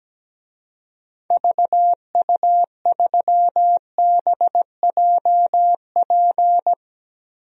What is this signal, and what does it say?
Morse code, heard as a keyed tone, about 17 wpm, 700 Hz. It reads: VU3BJP
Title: DXCC-100 Call Signs - 6 Characters - 17wpm